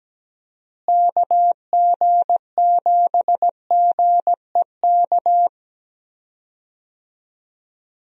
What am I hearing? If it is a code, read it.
Morse code, heard as a keyed tone, about 17 wpm, 700 Hz. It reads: KG7GEK
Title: USA Call Signs - 6 Characters - Repeated 2x - 17wpm